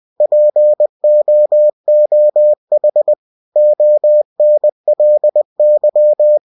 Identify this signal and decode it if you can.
Morse code, heard as a keyed tone, about 20 wpm, 600 Hz. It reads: POOH ONLY
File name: cw-books / winnie-the-pooh / winnie-20wpm-page-0019